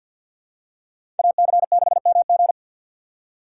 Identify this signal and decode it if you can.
Morse code, heard as a keyed tone, about 47 wpm, 700 Hz. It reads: AC6MG
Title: USA Call Signs - 4x Character Spacing - 50wpm